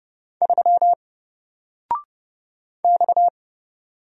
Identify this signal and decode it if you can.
Morse code, heard as a keyed tone, about 30 wpm, 700 Hz. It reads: 3 X
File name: Single-Letter-Number-30wpm